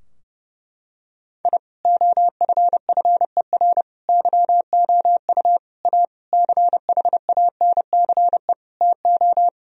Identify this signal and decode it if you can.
Morse code, heard as a keyed tone, about 30 wpm, 700 Hz. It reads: I OFFER YOU A CHANCE TO
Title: Sentences from Top 1000 Words - Review Part 4 - 30wpm